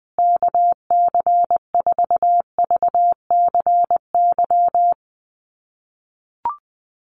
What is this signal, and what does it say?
Morse code, heard as a keyed tone, about 20 wpm, 700 Hz. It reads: KC4VCY
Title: USA Call Signs Repeated 2x 20wpm